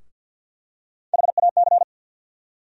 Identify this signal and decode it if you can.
Morse code, heard as a keyed tone, about 48 wpm, 700 Hz. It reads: HAC
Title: USA Call Sign Postfixes - 3 Letters 50wpm